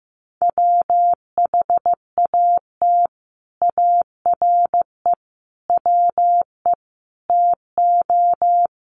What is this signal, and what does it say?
Morse code, heard as a keyed tone, about 15 wpm, 700 Hz. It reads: WHAT ARE WE TO
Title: Sentences from Top 100 Words 15wpm